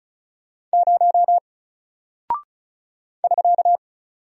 Morse code, heard as a keyed tone, about 36 wpm, 700 Hz. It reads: 0 <SK>
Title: Single Character (Letter:Number:Punctuation:Prosign) Alt 35wpm